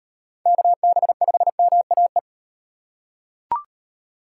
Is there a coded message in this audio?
Yes: Morse code from 0.5 to 2.2 s, heard as a keyed tone, about 35 wpm, 700 Hz. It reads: KB5MAE